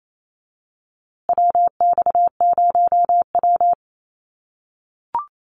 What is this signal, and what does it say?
Morse code, heard as a keyed tone, about 28 wpm, 700 Hz. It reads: WX0W
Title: USA Call Signs Repeated 2x 28wpm